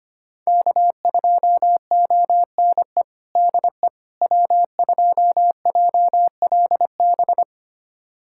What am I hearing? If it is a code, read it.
Morse code, heard as a keyed tone, about 25 wpm, 700 Hz. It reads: K2ONE DE W2JLB
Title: USA Callsign DE Callsign - 5 Characters - 25wpm